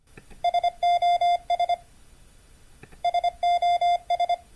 Morse code, heard as a keyed tone, about 25 wpm, 3500 Hz. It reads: SOS SOS